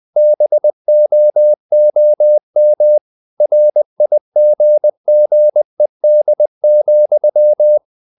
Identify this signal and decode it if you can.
Morse code, heard as a keyed tone, about 20 wpm, 600 Hz. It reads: BOOM RIGGED,
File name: the old man and the sea part 2 20wpm